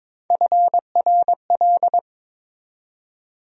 Morse code, heard as a keyed tone, about 22 wpm, 700 Hz. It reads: FRL
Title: USA Call Sign Postfixes - 3 Letters 22wpm